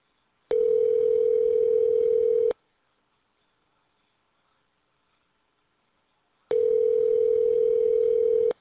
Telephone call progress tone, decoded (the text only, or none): ringback tone